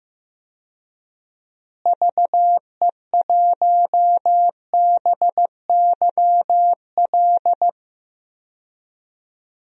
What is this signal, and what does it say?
Morse code, heard as a keyed tone, about 16 wpm, 700 Hz. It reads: VE1BYL
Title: DXCC-100 Call Signs - 6 Characters - Repeated 3x - 15wpm